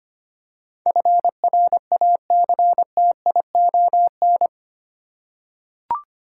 Morse code, heard as a keyed tone, about 25 wpm, 700 Hz. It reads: FRACTION
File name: 8 Letter Words - First 3 Letters Given - 25wpm